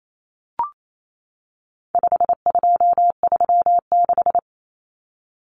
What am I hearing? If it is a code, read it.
Morse code, heard as a keyed tone, about 28 wpm, 700 Hz. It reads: 5236
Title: Numbers - 4-Digit 28wpm